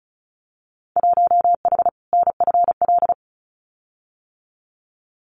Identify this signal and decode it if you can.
Morse code, heard as a keyed tone, about 34 wpm, 700 Hz. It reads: 1H NFL
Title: Winter Field Day Exchanges 35wpm